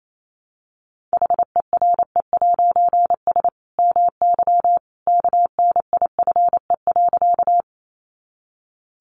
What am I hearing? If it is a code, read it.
Morse code, heard as a keyed tone, about 28 wpm, 700 Hz. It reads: HERE'S MY KNIFE.
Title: A Sound Of Thunder 28wpm